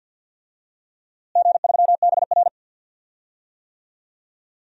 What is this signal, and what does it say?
Morse code, heard as a keyed tone, about 48 wpm, 700 Hz. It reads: G3BR
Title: DXCC-100 Call Signs - 4 Characters - Repeated 3x - 50wpm